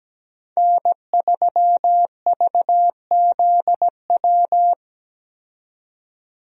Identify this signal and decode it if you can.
Morse code, heard as a keyed tone, about 17 wpm, 700 Hz. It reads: N3VZW
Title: USA Call Signs - 5 Characters - 17wpm